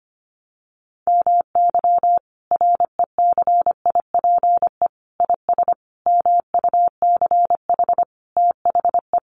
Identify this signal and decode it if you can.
Morse code, heard as a keyed tone, about 25 wpm, 700 Hz. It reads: MY RECIPE IS MUCH THE
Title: Sentences from Top 1200 Words 25wpm